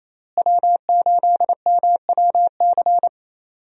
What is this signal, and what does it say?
Morse code, heard as a keyed tone, about 27 wpm, 700 Hz. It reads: W8MWC
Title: USA Call Signs - 5 Characters - Repeated 2x - 28wpm